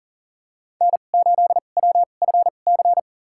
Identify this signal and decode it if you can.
Morse code, heard as a keyed tone, about 40 wpm, spaced as for 29 wpm, 700 Hz. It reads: N8WFC
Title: USA Call Signs - 2x Character Spacing - 40wpm